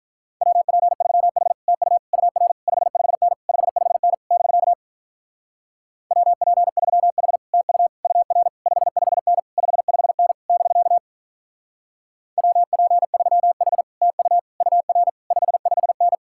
Morse code, heard as a keyed tone, about 44 wpm, 700 Hz. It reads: WP3H TU UR 55N 55N <BK> WP3H TU UR 55N 55N <BK> WP3H TU UR 55N